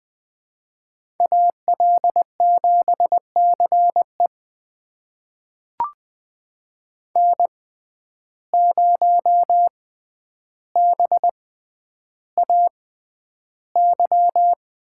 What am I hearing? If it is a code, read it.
Morse code, heard as a keyed tone, about 20 wpm, spaced as for 10 wpm, 700 Hz. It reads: AL7CE N0BAY
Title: USA Call Signs - 5 Characters - 6x Character Spacing - 20wpm